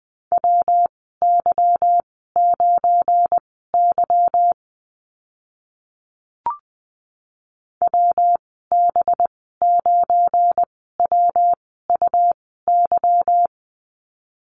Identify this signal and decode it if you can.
Morse code, heard as a keyed tone, about 20 wpm, spaced as for 14 wpm, 700 Hz. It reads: WY9Y WB9WUY
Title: 219 Lesson 35g - Number 0 - Call Signs 20wpm